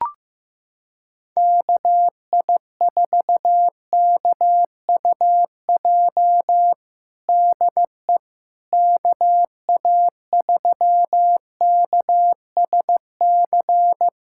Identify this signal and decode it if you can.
Morse code, heard as a keyed tone, about 16 wpm, 700 Hz. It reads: KI4KUJ DE KA3KSC
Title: USA Callsign DE Callsign 15wpm